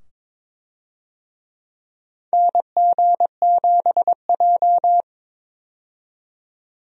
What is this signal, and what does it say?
Morse code, heard as a keyed tone, about 22 wpm, 700 Hz. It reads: NG7J
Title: USA Call Signs - 4 Characters - 22wpm